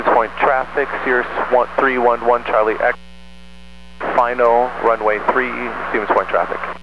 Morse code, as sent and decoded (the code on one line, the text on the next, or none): none